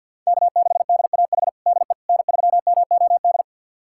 Morse code, heard as a keyed tone, about 52 wpm, 700 Hz. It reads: K6DAH DE N3MOD
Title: USA Callsign DE Callsign - 5 Characters - 50wpm